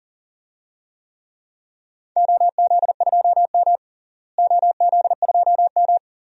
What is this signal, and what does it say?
Morse code, heard as a keyed tone, about 40 wpm, 700 Hz. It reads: OZ2M OZ2M
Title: DXCC-100 Call Signs - 4 Characters - Repeated 2x - 40wpm